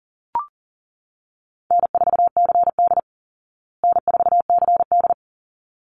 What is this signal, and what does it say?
Morse code, heard as a keyed tone, about 37 wpm, 700 Hz. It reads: N4CD N4CD
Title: Easy Call Signs Repeated 2x - 40wpm